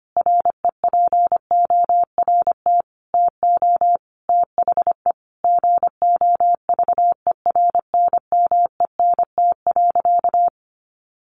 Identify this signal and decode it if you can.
Morse code, heard as a keyed tone, about 25 wpm, 700 Hz. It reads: REPORT TO THE GOVERNMENT.
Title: A Sound Of Thunder 25wpm